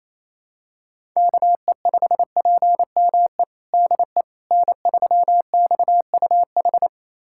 Morse code, heard as a keyed tone, about 28 wpm, 700 Hz. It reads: KE5PME DE N3XUH
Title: USA Callsign DE Callsign 28wpm